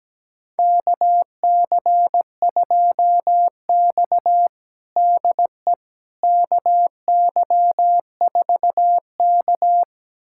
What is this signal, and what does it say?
Morse code, heard as a keyed tone, about 17 wpm, 700 Hz. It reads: KC2X DE KY4K
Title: USA Callsign DE Callsign - 4 Characters - 17wpm